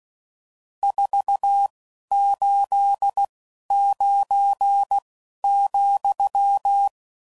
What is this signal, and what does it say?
Morse code, heard as a keyed tone, about 16 wpm, 800 Hz. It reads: 4 8 9 ,